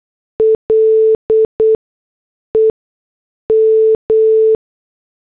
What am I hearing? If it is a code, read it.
Morse code, heard as a keyed tone, about 8 wpm, 440 Hz. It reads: L E M
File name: morse / headlines